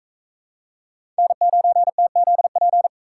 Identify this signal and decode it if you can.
Morse code, heard as a keyed tone, about 43 wpm, 700 Hz. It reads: N9TZP